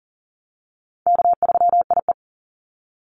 Morse code, heard as a keyed tone, about 40 wpm, 700 Hz. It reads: K3IE
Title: Call Signs Repeated 3x - 40wpm